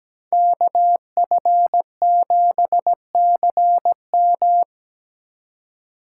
Morse code, heard as a keyed tone, about 17 wpm, 700 Hz. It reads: KF7CM
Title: USA Call Signs - 5 Characters - 4x Character Spacing - 17wpm